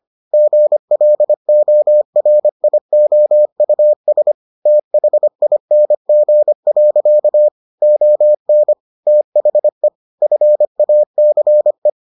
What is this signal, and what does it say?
Morse code, heard as a keyed tone, about 25 wpm, 600 Hz. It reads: GLORIOUS THING. ON THE FACE